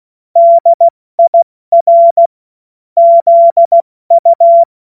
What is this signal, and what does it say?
Morse code, heard as a keyed tone, about 16 wpm, 680 Hz. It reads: DIR ZU
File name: Morsekurs / Witze / Witz 14. Februar 2023 70 F80